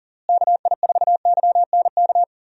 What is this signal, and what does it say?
Morse code, heard as a keyed tone, about 40 wpm, 700 Hz. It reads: KI4YNK
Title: USA Call Signs - 6 Characters - Repeated 3x - 40wpm